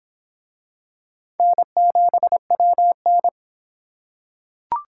Morse code, heard as a keyed tone, about 23 wpm, 700 Hz. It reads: N7WN